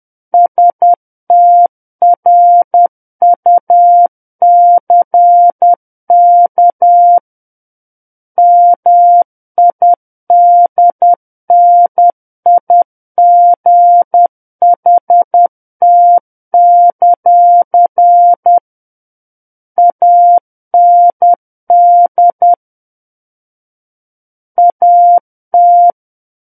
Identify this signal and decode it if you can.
Morse code, heard as a keyed tone, about 10 wpm, 700 Hz. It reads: STRUCK MIDNIGHT; AND AT
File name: Book One / 01-Chapter 1 - 10 WPM